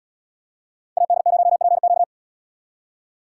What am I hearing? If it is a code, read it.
Morse code, heard as a keyed tone, about 75 wpm, 700 Hz. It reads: AD0PQ